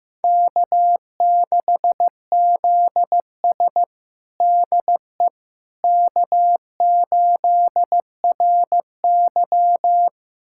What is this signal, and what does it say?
Morse code, heard as a keyed tone, about 15 wpm, 700 Hz. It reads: K6ZS DE K8RY
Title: USA Callsign DE Callsign - 4 Characters - 15wpm